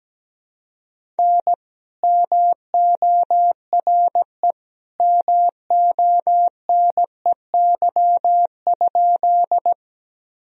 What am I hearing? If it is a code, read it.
Morse code, heard as a keyed tone, about 17 wpm, 700 Hz. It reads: N MORE MONEY?
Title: Sentences from Top 400 Words and Abbreviations 17wpm